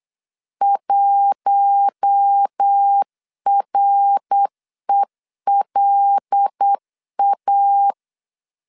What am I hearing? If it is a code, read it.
Morse code, heard as a keyed tone, about 8 wpm, 800 Hz. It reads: 1RELA
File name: 8wpm